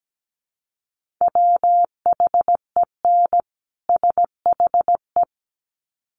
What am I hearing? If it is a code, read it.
Morse code, heard as a keyed tone, about 17 wpm, 700 Hz. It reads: WHEN SHE
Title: Sets of 2 Words 17wpm